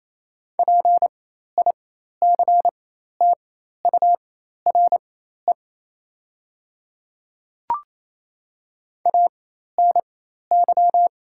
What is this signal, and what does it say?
Morse code, heard as a keyed tone, about 28 wpm, spaced as for 13 wpm, 690 Hz. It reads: PICTURE ANY
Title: 4x Character Spacing - Top 200 Words - 28wpm